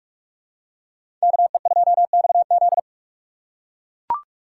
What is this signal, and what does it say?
Morse code, heard as a keyed tone, about 43 wpm, 700 Hz. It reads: KE2XZ
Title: USA Call Signs - 5 Characters - 4x Character Spacing - 45wpm